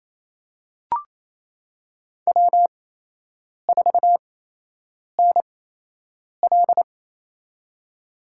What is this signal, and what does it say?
Morse code, heard as a keyed tone, about 28 wpm, spaced as for 8 wpm, 700 Hz. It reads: W4NL
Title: USA Call Signs - 4 Characters - 8x Character Spacing - 28wpm